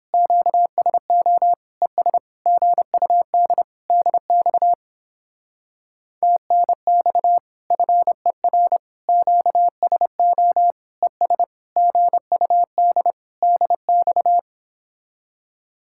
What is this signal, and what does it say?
Morse code, heard as a keyed tone, about 31 wpm, 700 Hz. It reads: QSO ES GUD DX TNX FER QSO ES GUD DX